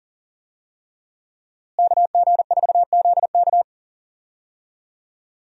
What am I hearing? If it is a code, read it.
Morse code, heard as a keyed tone, about 40 wpm, 700 Hz. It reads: KG4ZK
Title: USA Call Signs - 5 Characters - Repeated 3x - 40wpm